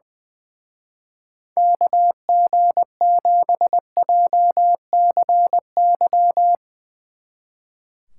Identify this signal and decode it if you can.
Morse code, heard as a keyed tone, about 20 wpm, 700 Hz. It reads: KG7JCY
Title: Speed Racing - USA Call Signs 20wpm